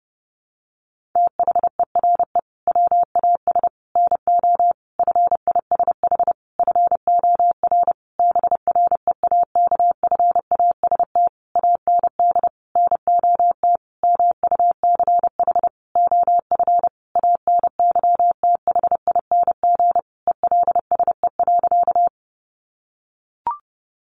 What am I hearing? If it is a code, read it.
Morse code, heard as a keyed tone, about 30 wpm, 700 Hz. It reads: THERE WAS NO FISH FOR BREAKFAST AND NOT MUCH OF ANYTHING ELSE.
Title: Sentences from Top 1500 Words 30wpm